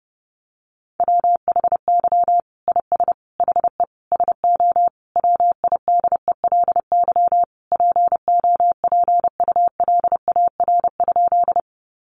Morse code, heard as a keyed tone, about 30 wpm, 700 Hz. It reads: WHY IS HE SO WIDELY POPULAR?
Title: Sentences from Top 1200 Words 30wpm